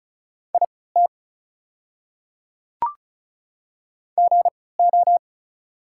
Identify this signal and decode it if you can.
Morse code, heard as a keyed tone, about 36 wpm, spaced as for 20 wpm, 700 Hz. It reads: IT GO
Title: 3x Character Spacing - Top 100 Words - 35wpm